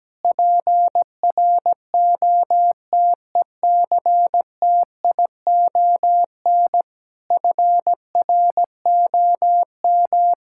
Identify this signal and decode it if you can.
Morse code, heard as a keyed tone, about 17 wpm, 700 Hz. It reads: PROTECTION FROM
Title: Sentences from Top 1600 Words 17wpm